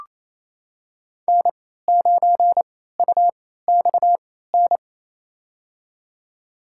Morse code, heard as a keyed tone, about 28 wpm, spaced as for 16 wpm, 700 Hz. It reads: N9UXN
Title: USA Call Signs - 3x Character Spacing - 28wpm